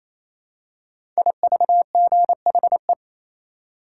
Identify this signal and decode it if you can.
Morse code, heard as a keyed tone, about 27 wpm, 700 Hz. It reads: IVGHE
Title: Code Groups - 5 Letters - 28wpm